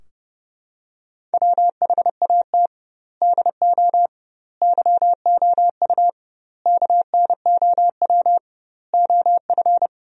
Morse code, heard as a keyed tone, about 30 wpm, 700 Hz. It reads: WHAT DO YOU KNOW OF
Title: 2x Word-Spacing - Sentences from Top 100 Words - 30wpm